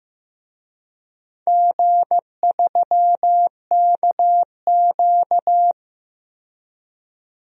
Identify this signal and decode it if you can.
Morse code, heard as a keyed tone, about 15 wpm, 700 Hz. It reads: G3KQ
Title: DXCC-100 Call Signs - 4 Characters - 15wpm